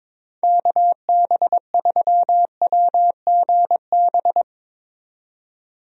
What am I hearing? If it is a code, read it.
Morse code, heard as a keyed tone, about 22 wpm, 700 Hz. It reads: KB3WGB
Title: USA Call Signs - 6 Characters - 22wpm